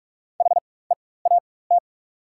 Morse code, heard as a keyed tone, about 44 wpm, spaced as for 21 wpm, 690 Hz. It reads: HEAT
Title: 4x Character Spacing - Top 500 Words - 45wpm